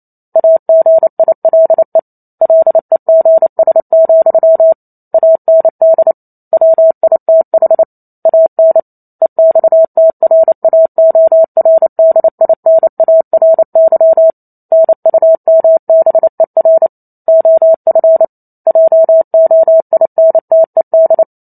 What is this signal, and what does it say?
Morse code, heard as a keyed tone, about 28 wpm, 650 Hz. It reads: AGILE LEGS, AND WITH AN EXTRAORDINARY NUMBER OF JOINTED